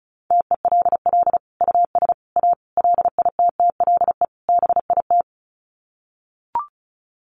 Morse code, heard as a keyed tone, about 34 wpm, 700 Hz. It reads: TELL US A LITTLE BIT
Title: Sets of 5 Words Encore2 35wpm